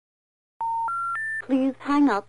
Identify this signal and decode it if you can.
Special information tone, with speech.